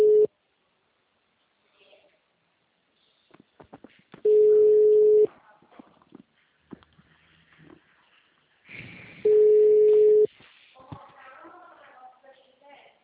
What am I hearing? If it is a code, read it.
Ringback tone (Europe, 425 Hz), with speech.
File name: out- -2004- - - .69091